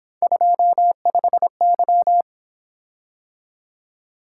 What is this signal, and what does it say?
Morse code, heard as a keyed tone, about 26 wpm, 700 Hz. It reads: 25Y
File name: Speed-Racing - Three Letter-Number Combinations - 22wpm